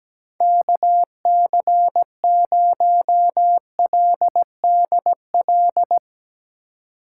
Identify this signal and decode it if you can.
Morse code, heard as a keyed tone, about 17 wpm, 700 Hz. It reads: KC0LDL
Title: Speed Racing - USA Call Signs 17wpm